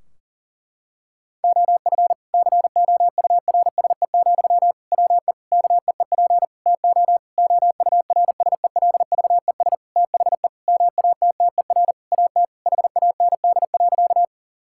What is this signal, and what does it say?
Morse code, heard as a keyed tone, about 40 wpm, 700 Hz. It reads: OF COURSE, WE KEEP TO OURSELVES THE MATTER AT HAND.
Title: Sentences from Top 1600 Words 40wpm